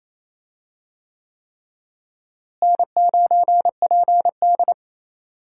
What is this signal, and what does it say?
Morse code, heard as a keyed tone, about 28 wpm, 700 Hz. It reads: N9PD
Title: USA Call Signs - 4 Characters - Repeated 3x - 28wpm